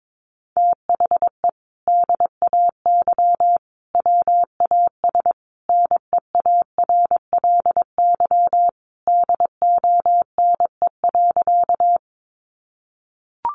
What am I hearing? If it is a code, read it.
Morse code, heard as a keyed tone, about 22 wpm, 700 Hz. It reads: THE DAY WAS NEARLY DONE.
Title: Sentences from Top 1300 Words 22wpm